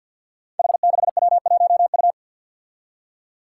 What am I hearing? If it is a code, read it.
Morse code, heard as a keyed tone, about 53 wpm, 700 Hz. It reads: H6W1U
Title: Five Challenging Letter-Number Combinations 50wpm